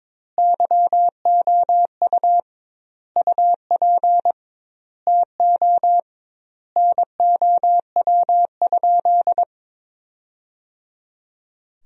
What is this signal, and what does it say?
Morse code, heard as a keyed tone, about 22 wpm, 700 Hz. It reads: YOU UP TO NOW?
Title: 2x Word-Spacing - Sentences from Top 100 Words - 22wpm